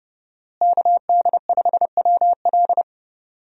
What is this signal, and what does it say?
Morse code, heard as a keyed tone, about 30 wpm, 700 Hz. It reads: KD5WL
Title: USA Call Signs - 5 Characters - Repeated 2x - 30wpm